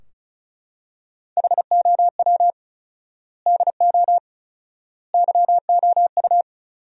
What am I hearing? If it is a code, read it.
Morse code, heard as a keyed tone, about 34 wpm, 700 Hz. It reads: HOW DO YOU
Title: 4x Word-Spacing - Sentences from Top 400 Words - 35wpm